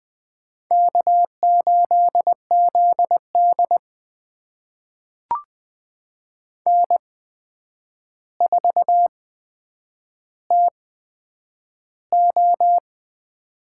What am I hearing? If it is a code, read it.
Morse code, heard as a keyed tone, about 20 wpm, spaced as for 9 wpm, 700 Hz. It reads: K8ZD N4TO
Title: USA Call Signs - 4 Characters - 8x Character Spacing - 20wpm